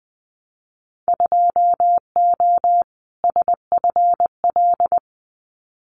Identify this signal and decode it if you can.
Morse code, heard as a keyed tone, about 20 wpm, 700 Hz. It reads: 2O SFL
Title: Winter Field Day Exchanges 20wpm